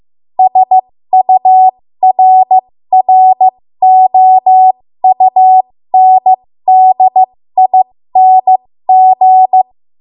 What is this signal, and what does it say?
Morse code, heard as a keyed tone, about 15 wpm, spaced as for 13 wpm, 750 Hz. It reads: SURROUNDING
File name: Morse / 13WPM